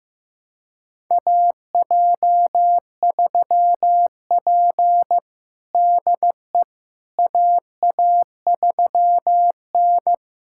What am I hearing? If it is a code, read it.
Morse code, heard as a keyed tone, about 15 wpm, 700 Hz. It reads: AJ3P DE AA3N
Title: USA Callsign DE Callsign - 4 Characters - 15wpm